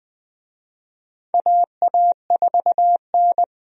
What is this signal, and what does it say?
Morse code, heard as a keyed tone, about 21 wpm, 700 Hz. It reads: AA4N
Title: USA Call Signs - 4 Characters - Repeated 3x - 20wpm